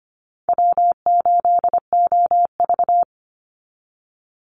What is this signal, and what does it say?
Morse code, heard as a keyed tone, about 25 wpm, 700 Hz. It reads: W8OV
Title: Easy Call Signs - 25wpm